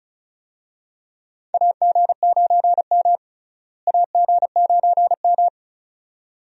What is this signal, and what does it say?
Morse code, heard as a keyed tone, about 36 wpm, 700 Hz. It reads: AG9M AG9M
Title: USA Call Signs - 4 Characters - Repeated 2x - 35wpm